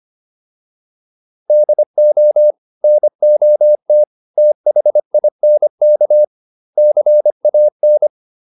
Morse code, heard as a keyed tone, about 25 wpm, 600 Hz. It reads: DO NOT THINK CAN